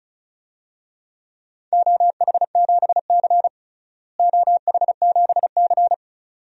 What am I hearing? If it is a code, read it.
Morse code, heard as a keyed tone, about 36 wpm, 700 Hz. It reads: OH7C OH7C
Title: DXCC-100 Call Signs - 4 Characters - Repeated 2x - 35wpm